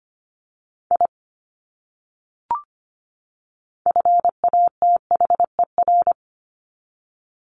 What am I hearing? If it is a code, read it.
Morse code, heard as a keyed tone, about 25 wpm, 700 Hz. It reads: I FATHER
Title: Top 200 Words 25wpm